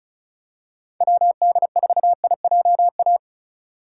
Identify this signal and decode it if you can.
Morse code, heard as a keyed tone, about 36 wpm, 700 Hz. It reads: WD4IJA
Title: USA Call Signs - 6 Characters - 4x Character Spacing - 35wpm